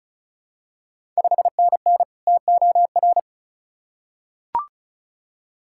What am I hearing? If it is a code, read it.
Morse code, heard as a keyed tone, about 36 wpm, 690 Hz. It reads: 5NN TOR
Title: Ontario QSO Party Exchange 35wpm